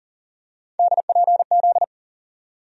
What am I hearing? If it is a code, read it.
Morse code, heard as a keyed tone, about 40 wpm, 700 Hz. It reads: DPZ
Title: USA Call Sign Postfixes - 3 Letters 40wpm